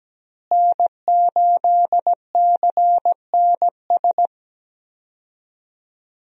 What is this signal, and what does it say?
Morse code, heard as a keyed tone, about 17 wpm, 700 Hz. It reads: N8CNS
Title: USA Call Signs - 5 Characters - Repeated 3x - 17wpm